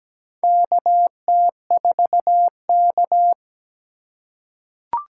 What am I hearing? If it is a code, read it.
Morse code, heard as a keyed tone, about 17 wpm, 700 Hz. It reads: KT4K
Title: USA Call Signs - 4x Character Spacing - 17wpm